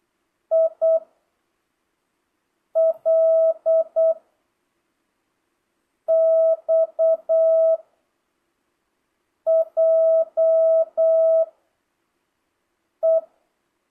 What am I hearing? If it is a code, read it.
Morse code, heard as a keyed tone, about 8 wpm, 650 Hz. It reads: I L X J E
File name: jh8cbh / 2025cw